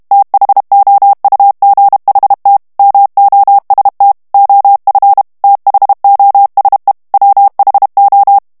Morse code, heard as a keyed tone, about 32 wpm, 800 Hz. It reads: THOUGHT MOST OF THOSE WHO
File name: ANGLAIS texte clair 2 vitesse 30 mots minute